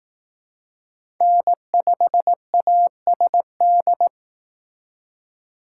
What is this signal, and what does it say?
Morse code, heard as a keyed tone, about 18 wpm, 700 Hz. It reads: N5ASD
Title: Speed Racing - USA 1x3 Call Signs 15wpm